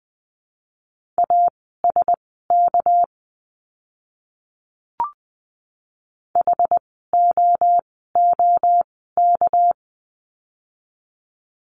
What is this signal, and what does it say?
Morse code, heard as a keyed tone, about 20 wpm, spaced as for 15 wpm, 700 Hz. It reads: ASK HOOK
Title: 199 Lesson 32f - Letter k - Words 20wpm